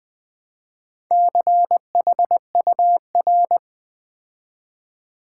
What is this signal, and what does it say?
Morse code, heard as a keyed tone, about 20 wpm, 700 Hz. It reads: CHUR
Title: Build A Word - 2 Letters At A Time - 6-Letter Words - 20wpm